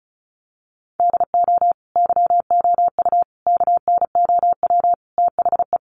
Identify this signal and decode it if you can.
Morse code, heard as a keyed tone, about 34 wpm, 700 Hz. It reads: DO YOU KNOW THE
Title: Sentences from Top 1000 Words 35wpm